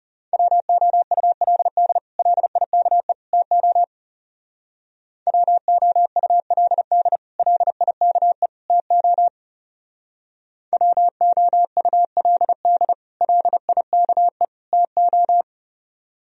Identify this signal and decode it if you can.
Morse code, heard as a keyed tone, about 38 wpm, 700 Hz. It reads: WOULD LIKE TO WOULD LIKE TO WOULD LIKE TO
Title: Speed-Racing - Sets of 3 Words 30wpm